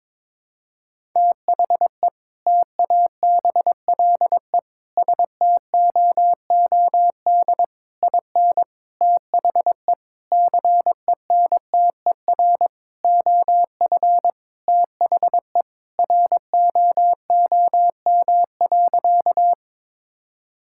Morse code, heard as a keyed tone, about 22 wpm, 700 Hz. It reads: THE TABLE STOOD IN THE CENTER OF THE ROOM.
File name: practice / 22wpm / Sentences with 4-Word Phrases - 22wpm